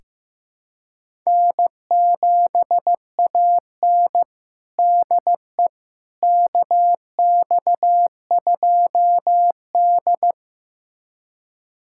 Morse code, heard as a keyed tone, about 16 wpm, 700 Hz. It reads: N7AN DE KX2D
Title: USA Callsign DE Callsign - 4 Characters - 15wpm